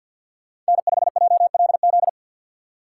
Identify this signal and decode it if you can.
Morse code, heard as a keyed tone, about 60 wpm, 700 Hz. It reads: N5JLZ
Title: Speed Racing - USA 1x3 Call Signs 50wpm